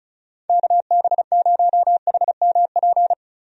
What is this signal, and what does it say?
Morse code, heard as a keyed tone, about 36 wpm, 700 Hz. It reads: KB0HMP
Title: USA Call Signs - 6 Characters - 35wpm